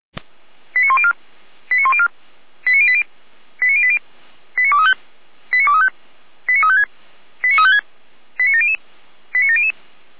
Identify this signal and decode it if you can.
Five-tone selective call (ZVEI), heard as quick runs of five tones, tones of about 70 ms. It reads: 89104 89104 89080 89080 89235 89235 89356 89356 89700 89700
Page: /455/SDR/ZVEI/89262/